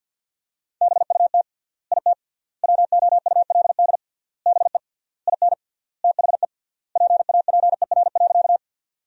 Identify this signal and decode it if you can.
Morse code, heard as a keyed tone, about 47 wpm, 700 Hz. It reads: BUT IT WOULD BE IN THE PAPER.
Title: 3x Word-Spacing - Sentences from Top 300 Words - 50wpm